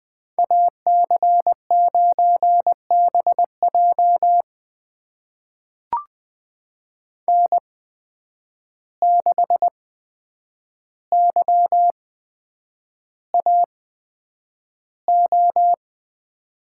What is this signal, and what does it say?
Morse code, heard as a keyed tone, about 20 wpm, spaced as for 9 wpm, 700 Hz. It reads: AC9BJ N6YAO
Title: USA Call Signs - 5 Characters - 8x Character Spacing - 20wpm